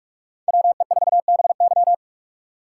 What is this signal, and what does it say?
Morse code, heard as a keyed tone, about 62 wpm, spaced as for 54 wpm, 700 Hz. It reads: WE4BY
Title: Speed Racing - USA 2x2 Call Signs 45wpm